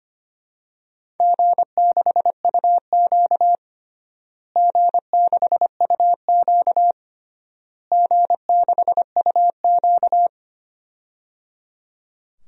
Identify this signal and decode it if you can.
Morse code, heard as a keyed tone, about 25 wpm, 700 Hz. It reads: G6UQ G6UQ G6UQ
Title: DXCC-100 Call Signs - 4 Characters - Repeated 3x - 25wpm